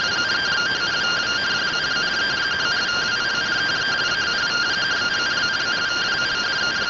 Two-tone teleprinter signal, about 170 Hz shift.